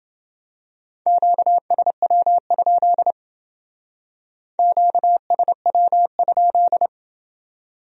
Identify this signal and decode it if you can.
Morse code, heard as a keyed tone, about 29 wpm, 700 Hz. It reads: QSW? QSW?